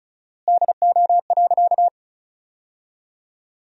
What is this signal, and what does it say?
Morse code, heard as a keyed tone, about 36 wpm, 700 Hz. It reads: DO.
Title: 4x Word-Spacing - Sentences from Top 100 Words - 35wpm